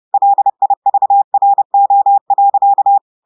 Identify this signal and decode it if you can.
Morse code, heard as a keyed tone, about 30 wpm, 800 Hz. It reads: LIVRO.
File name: audios / portugais / portugais texte en clair 1 vitesse 30 mots minutes